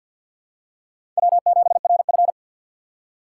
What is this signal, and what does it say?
Morse code, heard as a keyed tone, about 48 wpm, 700 Hz. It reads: W7RF
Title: USA Call Signs - 4 Characters - 6x Character Spacing - 50wpm